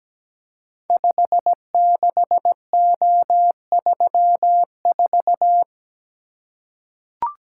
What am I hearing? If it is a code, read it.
Morse code, heard as a keyed tone, about 17 wpm, 700 Hz. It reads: 56O34
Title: Code Groups - 5 Letter:Numbers - 17wpm